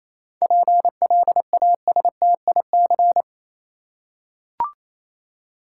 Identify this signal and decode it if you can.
Morse code, heard as a keyed tone, about 28 wpm, 700 Hz. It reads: PLASTIC
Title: Seven Letter Words - 28wpm